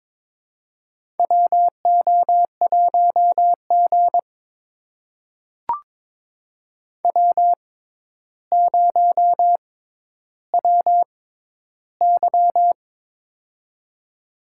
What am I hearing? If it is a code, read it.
Morse code, heard as a keyed tone, about 22 wpm, spaced as for 11 wpm, 700 Hz. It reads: WO1G W0WY
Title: USA Call Signs - 4 Characters - 6x Character Spacing - 22wpm